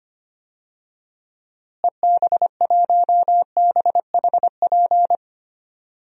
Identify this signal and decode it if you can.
Morse code, heard as a keyed tone, about 25 wpm, 700 Hz. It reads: EB1BHP
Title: DXCC-100 Call Signs - 6 Characters - Repeated 2x - 25wpm